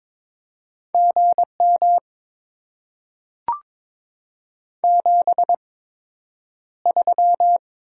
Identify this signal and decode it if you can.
Morse code, heard as a keyed tone, about 22 wpm, spaced as for 9 wpm, 700 Hz. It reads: GM 73
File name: Top 20 QSO Elements - 8x Character Spacing - 22wpm